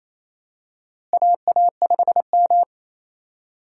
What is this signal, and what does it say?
Morse code, heard as a keyed tone, about 28 wpm, 700 Hz. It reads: AA5M
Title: USA Call Signs - 4 Characters - Repeated 2x - 28wpm